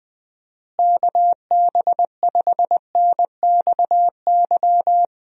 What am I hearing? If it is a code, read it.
Morse code, heard as a keyed tone, about 19 wpm, 700 Hz. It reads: KB5NXY
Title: USA Call Signs - 6 Characters - Repeated 3x - 20wpm